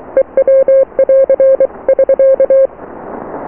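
Morse code, heard as a keyed tone, about 23 wpm, 540 Hz. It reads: EW <AR> <SK>
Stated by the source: Morse QSO on 14 MHz with a German station, received at RST 579.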